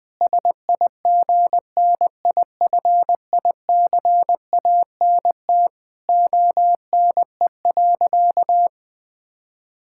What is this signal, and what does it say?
Morse code, heard as a keyed tone, about 20 wpm, 700 Hz. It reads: SIGNIFICANT ONE.
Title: Sentences from Top 1000 Words - Review Part 6 - 20wpm